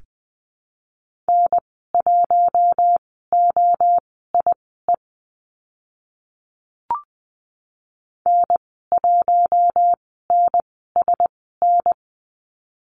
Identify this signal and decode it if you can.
Morse code, heard as a keyed tone, about 20 wpm, spaced as for 14 wpm, 700 Hz. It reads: N1OIE N1NSN
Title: 042 Lesson 8g - Number 1 - Call Signs 20wpm